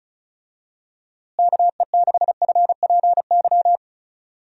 Morse code, heard as a keyed tone, about 34 wpm, 700 Hz. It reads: KE6FPY
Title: USA Call Signs - 6 Characters - Repeated 3x - 35wpm